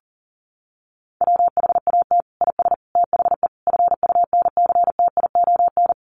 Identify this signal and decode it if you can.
Morse code, heard as a keyed tone, about 40 wpm, 700 Hz. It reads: WHAT IS THE FUNCTION
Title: Sentences from Top 1100 Words 40wpm